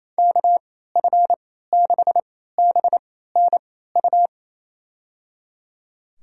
Morse code, heard as a keyed tone, about 28 wpm, spaced as for 16 wpm, 700 Hz. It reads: KF6BNU
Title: USA Call Signs - 3x Character Spacing - 28wpm